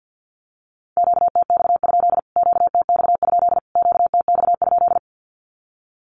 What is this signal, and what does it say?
Morse code, heard as a keyed tone, about 50 wpm, 700 Hz. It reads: QTX? QTX? QTX?
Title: Q-Codes Repeated 3x 50wpm